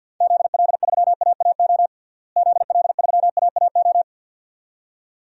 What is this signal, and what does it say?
Morse code, heard as a keyed tone, about 50 wpm, 700 Hz. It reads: ZL3AAO ZL3AAO
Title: DXCC-100 Call Signs - 6 Characters - Repeated 2x - 50wpm